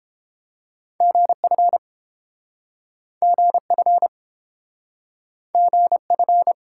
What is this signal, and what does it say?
Morse code, heard as a keyed tone, about 30 wpm, 700 Hz. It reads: GF GF GF